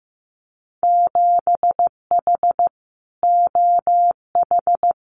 Morse code, heard as a keyed tone, about 15 wpm, 700 Hz. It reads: 7H OH
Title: Winter Field Day Exchanges Encore 15wpm